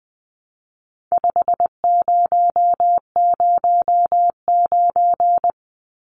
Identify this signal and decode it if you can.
Morse code, heard as a keyed tone, about 20 wpm, 700 Hz. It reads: 5009
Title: Numbers - 4-Digit 20wpm